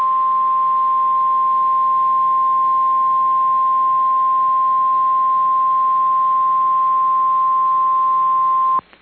Emergency Alert System (SAME) tones (the 1050 Hz attention tone). It is heard from 1.1 to 8.7 s.